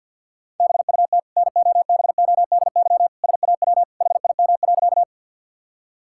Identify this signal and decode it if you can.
Morse code, heard as a keyed tone, about 50 wpm, 700 Hz. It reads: BUT NOBODY SAW HIM.
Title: Sentences from Top 1600 Words 50wpm